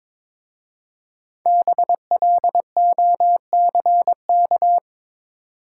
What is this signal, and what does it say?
Morse code, heard as a keyed tone, about 22 wpm, 700 Hz. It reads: BLOCK
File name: 5 Letter Words - First 3 Letters Given - 22wpm